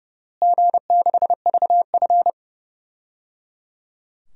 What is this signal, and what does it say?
Morse code, heard as a keyed tone, about 30 wpm, 700 Hz. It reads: G6VF
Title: DXCC-100 Call Signs - 4 Characters - Repeated 3x - 30wpm